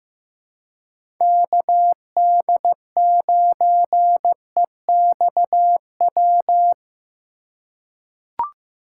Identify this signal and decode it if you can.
Morse code, heard as a keyed tone, about 15 wpm, 700 Hz. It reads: KD9EXW
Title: USA Call Signs - 2x Character Spacing - 15wpm